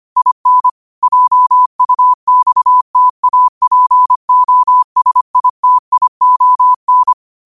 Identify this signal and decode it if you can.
Morse code, heard as a keyed tone, about 25 wpm, 1000 Hz. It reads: IN JUXTAPOSITION
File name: qotdfiles / 25wpm